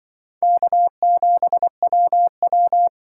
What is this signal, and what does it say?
Morse code, heard as a keyed tone, about 24 wpm, 700 Hz. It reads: K7WW
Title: Speed Racing - USA 1x2 Call Signs 20wpm